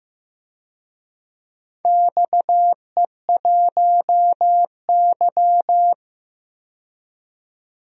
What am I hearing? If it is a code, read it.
Morse code, heard as a keyed tone, about 15 wpm, 700 Hz. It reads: XE1Y
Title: DXCC-100 Call Signs - 4 Characters - 15wpm